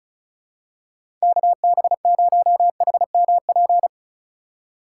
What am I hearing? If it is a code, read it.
Morse code, heard as a keyed tone, about 36 wpm, 700 Hz. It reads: KB0HMP
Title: USA Call Signs - 6 Characters - 2x Character Spacing - 35wpm